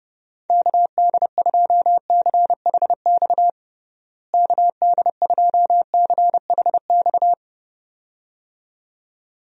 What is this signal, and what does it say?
Morse code, heard as a keyed tone, about 30 wpm, 700 Hz. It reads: KD2CHX KD2CHX
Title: USA Call Signs - 6 Characters - Repeated 2x - 30wpm